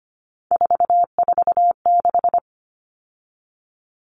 Morse code, heard as a keyed tone, about 25 wpm, 690 Hz. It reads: 446
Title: Numbers - 3-Digit 25wpm